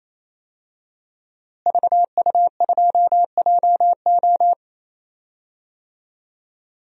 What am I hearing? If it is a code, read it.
Morse code, heard as a keyed tone, about 28 wpm, 700 Hz. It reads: VU2JO
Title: DXCC-100 Call Signs - 5 Characters - Repeated 3x - 28wpm